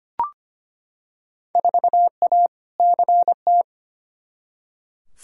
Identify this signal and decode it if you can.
Morse code, heard as a keyed tone, about 25 wpm, 700 Hz. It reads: 4A CT
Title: Field Day Exchanges Encore - 25wpm